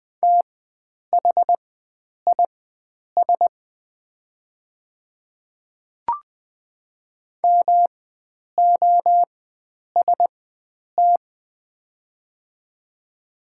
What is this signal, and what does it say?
Morse code, heard as a keyed tone, about 20 wpm, spaced as for 9 wpm, 690 Hz. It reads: THIS MOST
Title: 4x Character Spacing - Top 100 Words - 20wpm